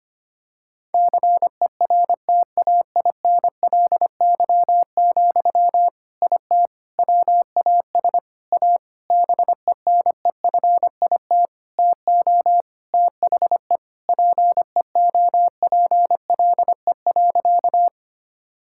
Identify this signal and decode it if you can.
Morse code, heard as a keyed tone, about 25 wpm, 700 Hz. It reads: CERTAINLY, IT WAS A BENEFIT TO THE PEOPLE.
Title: Sentences from Top 1000 Words - Review Part 6 - 25wpm